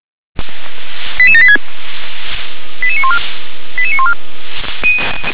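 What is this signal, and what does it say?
Five-tone selective call (ZVEI), heard as quick runs of five tones, tones of about 70 ms. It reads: 88786 88914 88914
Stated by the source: ZVEI five-tone call 88786 received on SDR.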